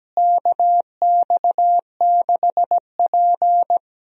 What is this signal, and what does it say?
Morse code, heard as a keyed tone, about 17 wpm, 700 Hz. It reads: KX6P
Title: USA Call Signs - 4 Characters - Repeated 3x - 17wpm